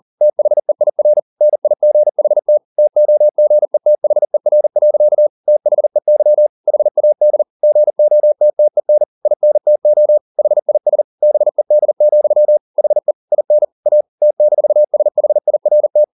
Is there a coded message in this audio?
Yes: Morse code from 0.2 to 16.0 s, heard as a keyed tone, about 40 wpm, 600 Hz. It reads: THEIR NIGHT TOGETHER. THEY HAD GOTTEN INTO HIS BED, HE IN A T-SHIRT